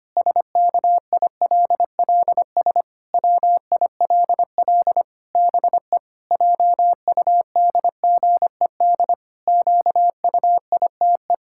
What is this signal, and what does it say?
Morse code, heard as a keyed tone, about 25 wpm, 700 Hz. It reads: SKILLS WILL BE JUDGED QUITE
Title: Pangram Challenge v2 - 25wpm